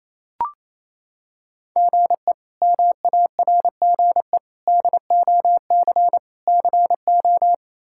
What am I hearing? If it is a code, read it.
Morse code, heard as a keyed tone, about 28 wpm, 700 Hz. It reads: GE MARGE DOC CO
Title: K1USN SST QSO Practice - 28wpm